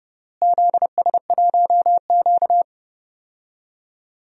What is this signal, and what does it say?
Morse code, heard as a keyed tone, about 30 wpm, 700 Hz. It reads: ZS1Q
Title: DXCC-100 Call Signs - 4 Characters - Repeated 3x - 30wpm